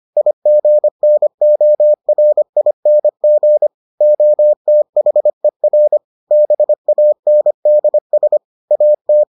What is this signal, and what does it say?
Morse code, heard as a keyed tone, about 25 wpm, 600 Hz. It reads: IGNORING OTHER BANDS AT